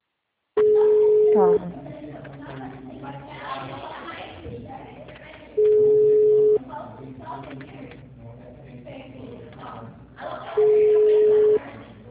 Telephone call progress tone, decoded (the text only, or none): ringback tone